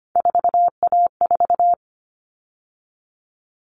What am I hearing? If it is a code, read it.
Morse code, heard as a keyed tone, about 25 wpm, 700 Hz. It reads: 4A4
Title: All Cut Numbers - 3-Digit 25wpm